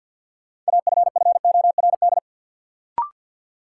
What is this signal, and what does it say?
Morse code, heard as a keyed tone, about 52 wpm, 700 Hz. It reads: AFFORD
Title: Sentences from Top 1100 Words - New Word Drill Alt - 50wpm